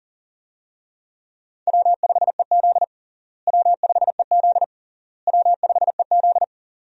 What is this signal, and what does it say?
Morse code, heard as a keyed tone, about 40 wpm, 700 Hz. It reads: W5EZ W5EZ W5EZ
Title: USA Call Signs - 4 Characters - Repeated 3x - 40wpm